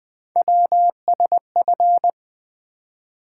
Morse code, heard as a keyed tone, about 20 wpm, 700 Hz. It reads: WSF
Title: Three Letter-Number Combinations - 20wpm